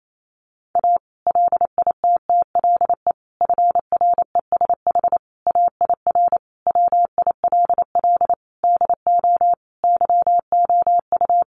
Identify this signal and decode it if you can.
Morse code, heard as a keyed tone, about 28 wpm, 700 Hz. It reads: A LITTLE FRESH AIR WILL DO YOU
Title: Sentences from Top 1200 Words 28wpm